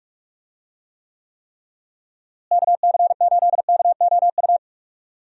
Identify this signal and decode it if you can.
Morse code, heard as a keyed tone, about 42 wpm, 700 Hz. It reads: KC8KOU
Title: USA Call Signs - 6 Characters - Repeated 3x - 45wpm